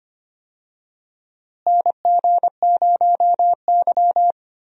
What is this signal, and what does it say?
Morse code, heard as a keyed tone, about 25 wpm, 700 Hz. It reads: NG0Y
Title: USA Call Signs - 4 Characters - Repeated 2x - 25wpm